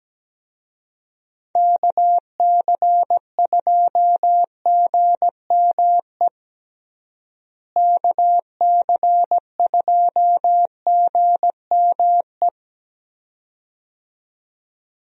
Morse code, heard as a keyed tone, about 17 wpm, 700 Hz. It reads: KC2GME KC2GME
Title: USA Call Signs - 6 Characters - Repeated 2x - 17wpm